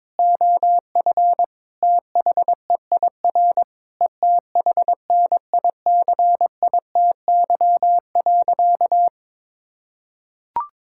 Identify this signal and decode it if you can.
Morse code, heard as a keyed tone, about 22 wpm, 700 Hz. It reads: OF THEIR ETHNICITY.
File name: Sentences with 9-Word Phrases - 22wpm